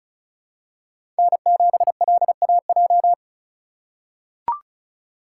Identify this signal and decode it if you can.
Morse code, heard as a keyed tone, about 36 wpm, 700 Hz. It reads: N7LAJ
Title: USA Call Signs - 5 Characters - 2x Character Spacing - 35wpm